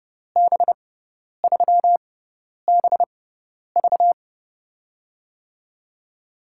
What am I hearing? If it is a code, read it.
Morse code, heard as a keyed tone, about 30 wpm, spaced as for 10 wpm, 700 Hz. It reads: B3BV
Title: USA Call Signs - 6x Character Spacing - 30wpm